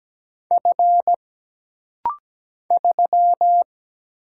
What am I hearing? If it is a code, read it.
Morse code, heard as a keyed tone, about 17 wpm, 700 Hz. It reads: F 3
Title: Single Letter-Number Alt - Rapid-Fire 17wpm